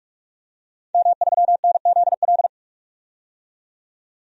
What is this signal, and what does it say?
Morse code, heard as a keyed tone, about 46 wpm, 700 Hz. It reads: M3NZL
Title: DXCC-100 Call Signs - 5 Characters - 45wpm